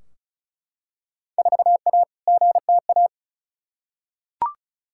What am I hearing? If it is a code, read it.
Morse code, heard as a keyed tone, about 34 wpm, 700 Hz. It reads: 4A GTA
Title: Field Day Exchanges Encore - 35wpm